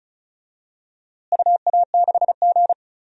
Morse code, heard as a keyed tone, about 36 wpm, 700 Hz. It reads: UA6G
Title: DXCC-100 Call Signs - 4 Characters - Repeated 3x - 35wpm